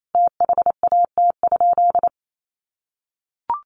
Morse code, heard as a keyed tone, about 28 wpm, 700 Hz. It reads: THAT?
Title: Sentences from Top 1100 Words 28wpm